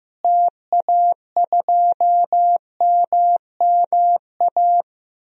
Morse code, heard as a keyed tone, about 16 wpm, 700 Hz. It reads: TA2MMA
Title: DXCC-100 Call Signs - 6 Characters - Repeated 2x - 15wpm